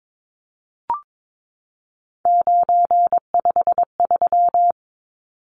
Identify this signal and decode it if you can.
Morse code, heard as a keyed tone, about 22 wpm, 700 Hz. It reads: 953
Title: Numbers - 3-Digit 22wpm